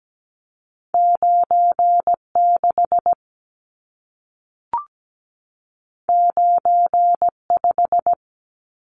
Morse code, heard as a keyed tone, about 17 wpm, 700 Hz. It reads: 96 95
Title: Numbers - 2-Digit 17wpm